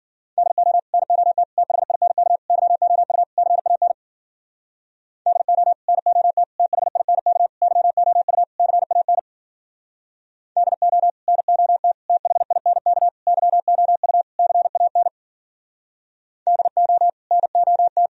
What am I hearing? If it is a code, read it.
Morse code, heard as a keyed tone, about 52 wpm, 700 Hz. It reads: DO NOT THINK YOU CAN DO NOT THINK YOU CAN DO NOT THINK YOU CAN DO NOT